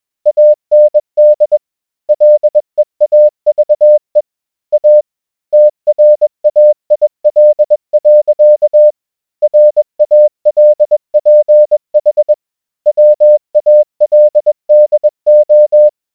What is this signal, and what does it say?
Morse code, heard as a keyed tone, about 20 wpm, 600 Hz. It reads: AND LEAVE A TRAIL. RALPH WALDO